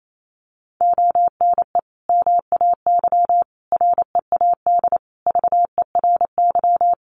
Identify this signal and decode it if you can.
Morse code, heard as a keyed tone, about 28 wpm, 700 Hz. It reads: ONE MAY READ VERY
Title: Sentences from Top 1100 Words 28wpm